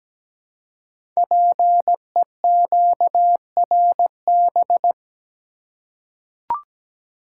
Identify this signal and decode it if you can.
Morse code, heard as a keyed tone, about 17 wpm, 700 Hz. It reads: PEQRB
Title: Code Groups - 5 Letters - 17wpm